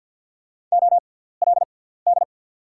Morse code, heard as a keyed tone, about 48 wpm, spaced as for 17 wpm, 690 Hz. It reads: OLD
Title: 6x Character Spacing - Top 300 Words - 50wpm